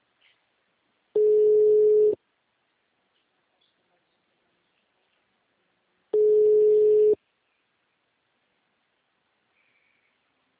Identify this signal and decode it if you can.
Ringback tone (Europe, 425 Hz).
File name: out- -2010- - - .25521